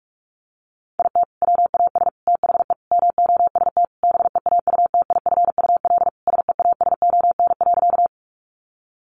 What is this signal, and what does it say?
Morse code, heard as a keyed tone, about 44 wpm, 700 Hz. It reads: IT WAS THE MOST BEAUTIFUL SEASON.
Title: Sentences from Top 1400 Words 45wpm